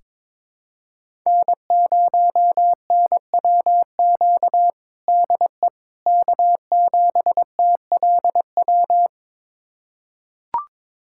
Morse code, heard as a keyed tone, about 21 wpm, 700 Hz. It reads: N0NWQ DE K7TLW
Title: USA Callsign DE Callsign - 5 Characters - 22wpm